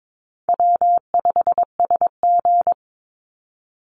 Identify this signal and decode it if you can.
Morse code, heard as a keyed tone, about 22 wpm, 700 Hz. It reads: W5SG
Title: Call Signs - 22wpm